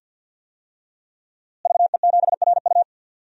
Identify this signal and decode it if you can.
Morse code, heard as a keyed tone, about 48 wpm, 700 Hz. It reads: VE7RU
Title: DXCC-100 Call Signs - 5 Characters - Repeated 2x - 50wpm